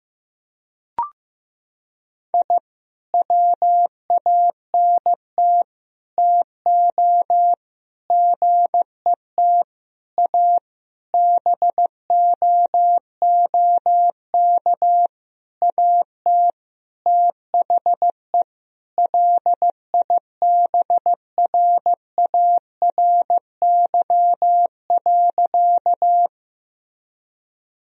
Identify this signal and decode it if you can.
Morse code, heard as a keyed tone, about 15 wpm, 700 Hz. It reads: I WANT TO GET A BOOK AT THE LIBRARY.
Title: Sentences from Top 1000 Words - Review Part 1 - 15wpm